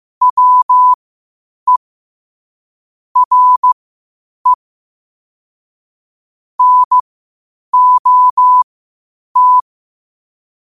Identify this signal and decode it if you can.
Morse code, heard as a keyed tone, about 15 wpm, spaced as for 8 wpm, 1000 Hz. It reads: WE RE NOT
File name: qotdfiles / 8wpm